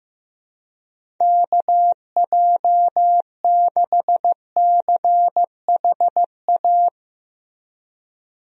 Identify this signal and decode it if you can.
Morse code, heard as a keyed tone, about 15 wpm, 700 Hz. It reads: KJ6CHA
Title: USA Call Signs - 6 Characters - Repeated 2x - 15wpm